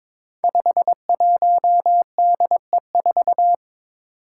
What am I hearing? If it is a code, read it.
Morse code, heard as a keyed tone, about 22 wpm, 700 Hz. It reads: 51DE4
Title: Five Challenging Letter-Number Combinations 22wpm